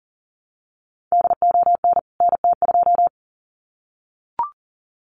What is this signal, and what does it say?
Morse code, heard as a keyed tone, about 40 wpm, 700 Hz. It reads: DON NT2
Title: CWT Exchange 40wpm